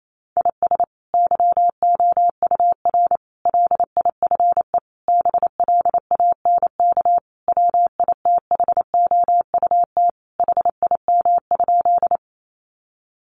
Morse code, heard as a keyed tone, about 28 wpm, 700 Hz. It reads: IS YOUR LIFE BLANK WITHOUT HIM?
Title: Sentences from Top 1500 Words 28wpm